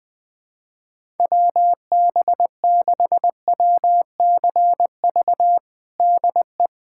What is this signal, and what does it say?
Morse code, heard as a keyed tone, about 21 wpm, 700 Hz. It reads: WB6WCV DE
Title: USA Callsign DE Callsign - 6 Characters - 20wpm